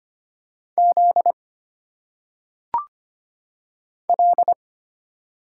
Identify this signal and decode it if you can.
Morse code, heard as a keyed tone, about 25 wpm, 700 Hz. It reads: Z L
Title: Single Letters - Phonetic Pronunciation 25wpm